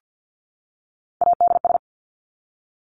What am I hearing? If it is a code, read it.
Morse code, heard as a keyed tone, about 50 wpm, 700 Hz. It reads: ADS
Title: Abbreviations 50wpm